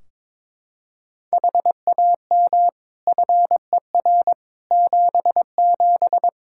Morse code, heard as a keyed tone, about 22 wpm, 700 Hz. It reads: HAM FER 77
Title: K1USN SST QSO Practice 22wpm